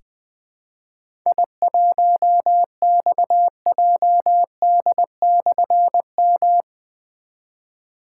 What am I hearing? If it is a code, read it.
Morse code, heard as a keyed tone, about 20 wpm, 700 Hz. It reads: I1XJD/M
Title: Difficult Call Signs 20wpm